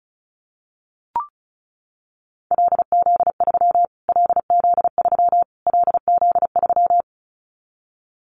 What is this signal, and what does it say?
Morse code, heard as a keyed tone, about 34 wpm, 700 Hz. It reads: LZ3 LZ3 LZ3
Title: Callsign Prefixes for Top 25 DXCC - Repeated 3x - 35wpm